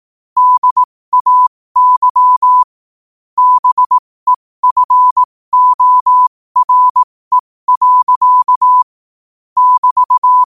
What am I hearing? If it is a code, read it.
Morse code, heard as a keyed tone, about 18 wpm, spaced as for 15 wpm, 1000 Hz. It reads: DAY BEFORE. <BT>
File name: qotdfiles / 15wpm